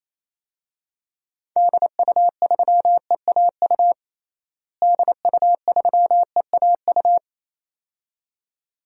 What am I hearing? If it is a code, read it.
Morse code, heard as a keyed tone, about 28 wpm, 700 Hz. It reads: DU3EAU DU3EAU
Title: DXCC-100 Call Signs - 6 Characters - Repeated 2x - 28wpm